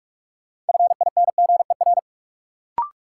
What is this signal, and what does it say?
Morse code, heard as a keyed tone, about 45 wpm, 700 Hz. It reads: FINGER
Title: Sentences from Top 1600 Words - New Word Drill Alt - 45wpm